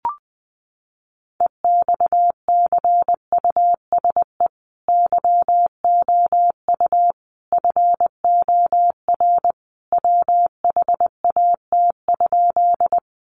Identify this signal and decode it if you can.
Morse code, heard as a keyed tone, about 20 wpm, 700 Hz. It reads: EXCUSE YOU FOR WHAT?
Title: Sentences from Top 700 Words 20wpm